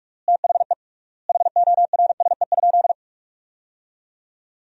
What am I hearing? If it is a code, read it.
Morse code, heard as a keyed tone, about 46 wpm, 700 Hz. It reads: THE HORSE?
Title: 3x Word-Spacing - Sentences from Top 500 Words - 45wpm